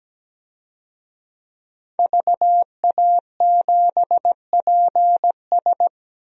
Morse code, heard as a keyed tone, about 17 wpm, 700 Hz. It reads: VA7PS
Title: DXCC-100 Call Signs - 5 Characters - 17wpm